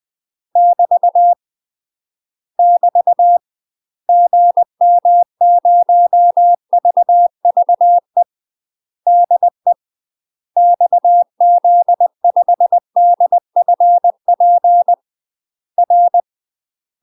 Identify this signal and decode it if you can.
Morse code, heard as a keyed tone, about 20 wpm, 700 Hz. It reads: <BT> <BT> GM0VVE DE XZ5DFP R
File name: UK MASTER QSOs TRAINING / 20WPM / UK MASTER QSOs 0062